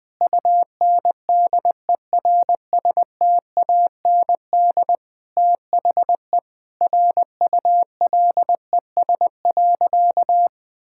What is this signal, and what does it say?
Morse code, heard as a keyed tone, about 20 wpm, 700 Hz. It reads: UNDERSTAND THE RULES.
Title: Sentences with 6-Word Phrases - 20wpm